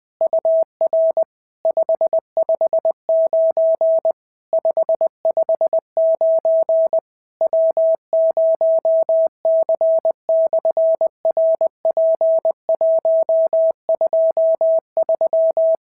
Morse code, heard as a keyed tone, about 20 wpm, 650 Hz. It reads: UR 559 559 W0C/RP123
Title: SOTA QSO Practice 20wpm